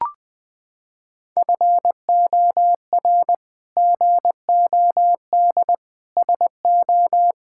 Morse code, heard as a keyed tone, about 20 wpm, 700 Hz. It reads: FOR GOD SO
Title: Sets of 8 Words - 20wpm